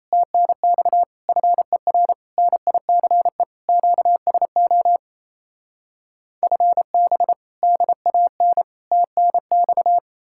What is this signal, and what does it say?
Morse code, heard as a keyed tone, about 38 wpm, 700 Hz. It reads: TNX FER NICE QSO FB DAN TNX